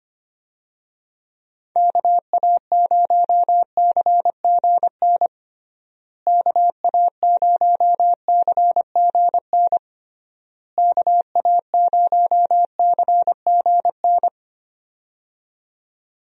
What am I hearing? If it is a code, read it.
Morse code, heard as a keyed tone, about 25 wpm, 700 Hz. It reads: KA0CGN KA0CGN KA0CGN
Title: USA Call Signs - 6 Characters - Repeated 3x - 25wpm